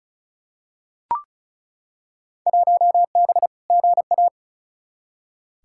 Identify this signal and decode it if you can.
Morse code, heard as a keyed tone, about 36 wpm, 700 Hz. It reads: 1B GA
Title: Field Day Exchanges Encore - 35wpm